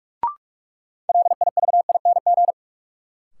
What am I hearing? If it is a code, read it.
Morse code, heard as a keyed tone, about 42 wpm, 700 Hz. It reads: LIVING
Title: Sets of 4 Words - New Word Drill - 45wpm